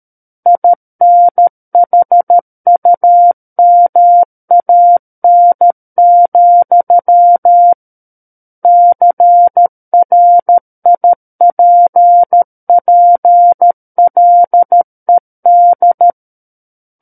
Morse code, heard as a keyed tone, about 13 wpm, 700 Hz. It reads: INHUMAN, CRIPPLED
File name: Book One / 04-Chapter 4 - 13 WPM